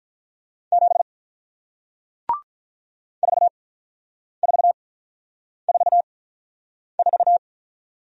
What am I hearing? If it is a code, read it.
Morse code, heard as a keyed tone, about 48 wpm, spaced as for 9 wpm, 700 Hz. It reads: 7 4444